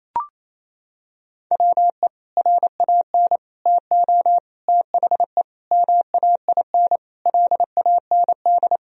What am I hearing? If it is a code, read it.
Morse code, heard as a keyed tone, about 28 wpm, 690 Hz. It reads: WE RAN TO THE MAIN LAND
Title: Sentences from Top 1000 Words - Review Part 2 - 28wpm